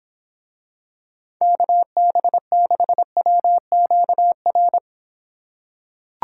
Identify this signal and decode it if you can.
Morse code, heard as a keyed tone, about 26 wpm, 700 Hz. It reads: KB6WQR